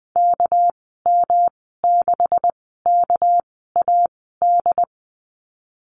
Morse code, heard as a keyed tone, about 20 wpm, spaced as for 14 wpm, 700 Hz. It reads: KM6KAD
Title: 219 Lesson 35g - Number 0 - Call Signs 20wpm